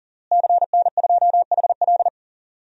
Morse code, heard as a keyed tone, about 40 wpm, 700 Hz. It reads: CN2HL
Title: DXCC-100 Call Signs - 5 Characters - Repeated 3x - 40wpm